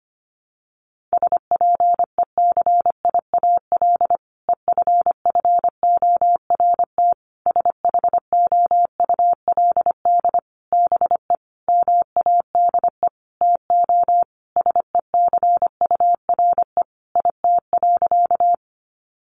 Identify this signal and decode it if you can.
Morse code, heard as a keyed tone, about 25 wpm, 700 Hz. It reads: SPECIAL EFFORT SHOULD BE MADE TO SECURE IT.
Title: Sentences from Top 1200 Words 25wpm